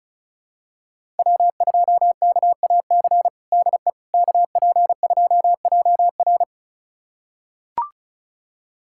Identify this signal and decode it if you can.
Morse code, heard as a keyed tone, about 36 wpm, 700 Hz. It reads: W2KAC DE KP2JR
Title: USA Callsign DE Callsign - 5 Characters - 35wpm